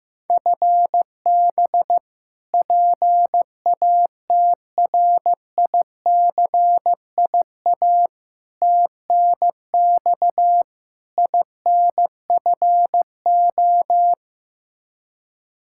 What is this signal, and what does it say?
Morse code, heard as a keyed tone, about 15 wpm, 700 Hz. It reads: FB PATRICIA TNX INFO
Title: Practice QSO - Exchange 3 - More Info - 15wpm